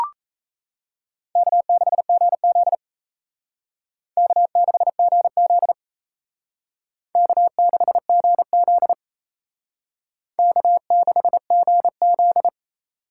Morse code, heard as a keyed tone, about 35 wpm, 700 Hz. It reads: K6GZ K6GZ K6GZ K6GZ
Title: Speed Racing - USA Call Signs 28wpm